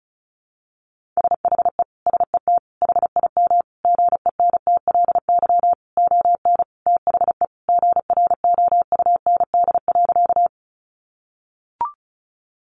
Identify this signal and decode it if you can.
Morse code, heard as a keyed tone, about 34 wpm, 700 Hz. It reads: SHE SET HIM GENTLY ON THE GROUND.
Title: Sentences from Top 1500 Words 35wpm